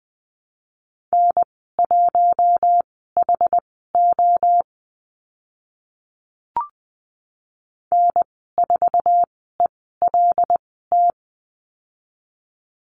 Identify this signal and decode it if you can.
Morse code, heard as a keyed tone, about 20 wpm, spaced as for 14 wpm, 700 Hz. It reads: N1HO N4ELT
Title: 088 Lesson 15g - Number 5 - Call Signs 20wpm